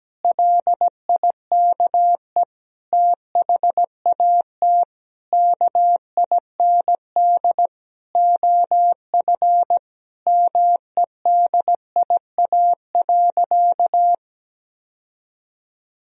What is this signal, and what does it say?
Morse code, heard as a keyed tone, about 17 wpm, 700 Hz. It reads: LIKE THAT KIND OF MEDIA.
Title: Sentences from Top 1000 Words - Review Part 4 - 17wpm